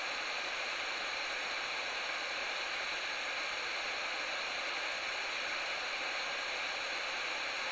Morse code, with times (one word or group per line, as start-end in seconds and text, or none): none